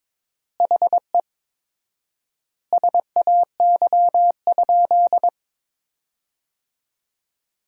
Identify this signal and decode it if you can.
Morse code, heard as a keyed tone, about 22 wpm, 700 Hz. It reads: HE SAY?
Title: 4x Word-Spacing - Sentences from Top 200 Words - 22wpm